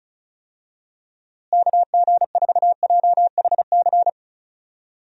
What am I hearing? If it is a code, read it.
Morse code, heard as a keyed tone, about 34 wpm, 700 Hz. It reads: KG4JHC
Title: USA Call Signs - 6 Characters - Repeated 2x - 35wpm